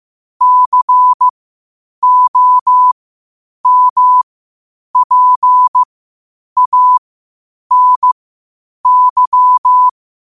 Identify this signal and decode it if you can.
Morse code, heard as a keyed tone, about 15 wpm, spaced as for 8 wpm, 1000 Hz. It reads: COMPANY
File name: qotdfiles / 8wpm